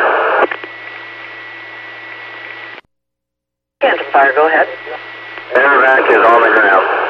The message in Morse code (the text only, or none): none